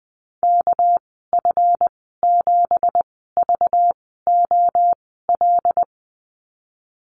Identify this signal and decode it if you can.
Morse code, heard as a keyed tone, about 20 wpm, spaced as for 15 wpm, 700 Hz. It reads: KF7VOL
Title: 213 Lesson 34g - Number 8 - Call Signs 20wpm